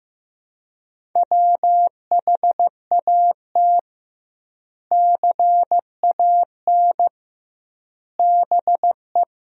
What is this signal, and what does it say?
Morse code, heard as a keyed tone, about 15 wpm, 700 Hz. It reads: WHAT CAN BE
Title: 2x Word-Spacing - Sentences from Top 500 Words - 15wpm